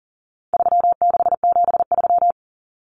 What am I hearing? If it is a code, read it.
Morse code, heard as a keyed tone, about 40 wpm, 700 Hz. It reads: 3673
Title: Common Cut Numbers - 4-Digit 40wpm